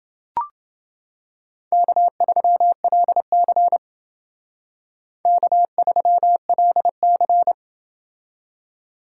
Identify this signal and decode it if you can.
Morse code, heard as a keyed tone, about 28 wpm, 700 Hz. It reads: K3LC K3LC